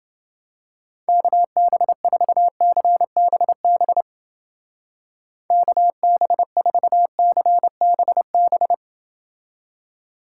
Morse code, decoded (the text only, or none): KB4CBB KB4CBB